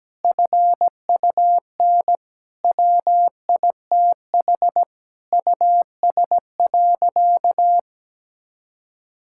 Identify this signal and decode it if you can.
Morse code, heard as a keyed tone, about 17 wpm, 700 Hz. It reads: FUN WITH US.
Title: Sentences from Top 1000 Words - Review Part 6 - 17wpm